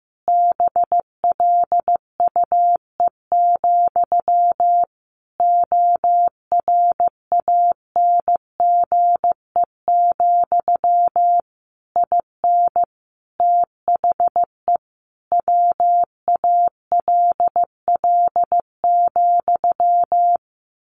Morse code, heard as a keyed tone, about 15 wpm, 700 Hz. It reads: BLUE, ORANGE, IN THE WALL,
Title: A Sound Of Thunder 15wpm